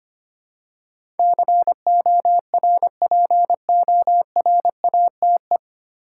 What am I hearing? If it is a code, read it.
Morse code, heard as a keyed tone, about 25 wpm, 700 Hz. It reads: CORPORATE
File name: Nine Letter Words - 25wpm